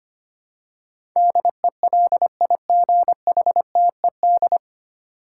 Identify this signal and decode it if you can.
Morse code, heard as a keyed tone, about 25 wpm, 700 Hz. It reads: DELIGHTED
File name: practice / 25wpm / 9 Letter Words - First 3 Letters Given - 25wpm